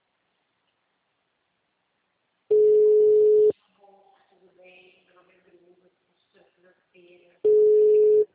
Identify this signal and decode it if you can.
Ringback tone (Europe, 425 Hz), with speech.